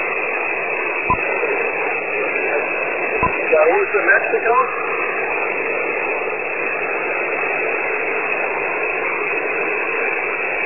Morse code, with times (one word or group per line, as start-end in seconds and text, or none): none